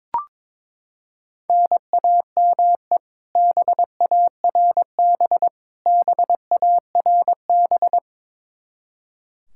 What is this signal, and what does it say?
Morse code, heard as a keyed tone, about 21 wpm, 700 Hz. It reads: NAME BARB BARB
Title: Practice QSO - Exchange 1 - Beginning - 22wpm